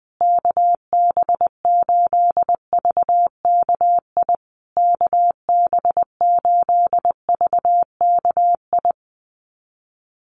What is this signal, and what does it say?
Morse code, heard as a keyed tone, about 20 wpm, 700 Hz. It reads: KB8VKI KB8VKI
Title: USA Call Signs Repeated 3x 20wpm